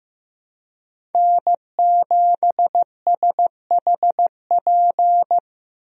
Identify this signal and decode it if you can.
Morse code, heard as a keyed tone, about 15 wpm, 700 Hz. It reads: N7SHP
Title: USA Call Signs - 5 Characters - Repeated 2x - 15wpm